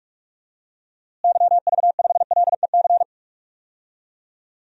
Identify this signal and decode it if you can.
Morse code, heard as a keyed tone, about 46 wpm, 700 Hz. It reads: YV5LEC
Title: DXCC-100 Call Signs - 6 Characters - Repeated 3x - 45wpm